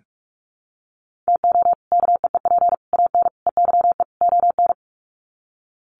Morse code, heard as a keyed tone, about 44 wpm, 700 Hz. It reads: TO KEEP AN EYE ON
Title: Sets of 5 Words Encore 45wpm